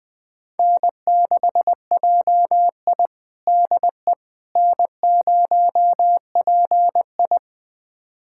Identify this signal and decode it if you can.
Morse code, heard as a keyed tone, about 20 wpm, 700 Hz. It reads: N6JI DE N0PI
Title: USA Callsign DE Callsign - 4 Characters - 20wpm